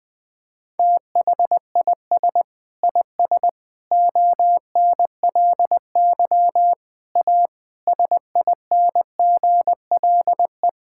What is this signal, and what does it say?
Morse code, heard as a keyed tone, about 20 wpm, 700 Hz. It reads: THIS IS ONLY A SINGLE
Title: Sentences from Top 1000 Words - Review Part 5 - 20wpm